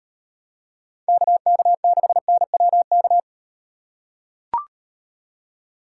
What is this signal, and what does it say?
Morse code, heard as a keyed tone, about 38 wpm, 700 Hz. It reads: KK6NWK